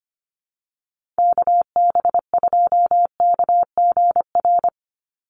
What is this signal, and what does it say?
Morse code, heard as a keyed tone, about 25 wpm, 700 Hz. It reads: KB2KGR
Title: USA Call Signs 25wpm